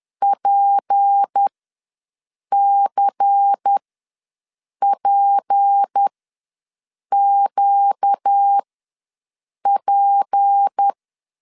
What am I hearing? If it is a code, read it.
Morse code, heard as a keyed tone, about 10 wpm, spaced as for 6 wpm, 800 Hz. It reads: PCPQP
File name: LEZIONI / 6 PKX0 / 6 P ws 10wpm 3751